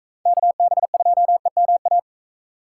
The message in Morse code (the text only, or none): KB2EMA